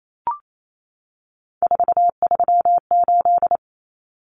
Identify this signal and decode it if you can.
Morse code, heard as a keyed tone, about 28 wpm, 700 Hz. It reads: 438
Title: Numbers - 3-Digit 28wpm